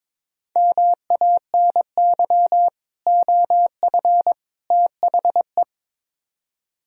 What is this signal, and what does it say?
Morse code, heard as a keyed tone, about 22 wpm, 700 Hz. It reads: MANY OF THE
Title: Speed-Racing - Sets of 3 Words 22wpm